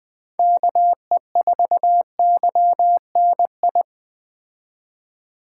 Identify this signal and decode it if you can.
Morse code, heard as a keyed tone, about 21 wpm, 700 Hz. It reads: KE4YNI